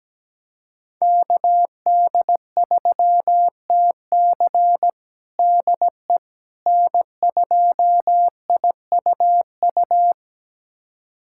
Morse code, heard as a keyed tone, about 17 wpm, 700 Hz. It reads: KD3TC DE N2IUU
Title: USA Callsign DE Callsign - 5 Characters - 17wpm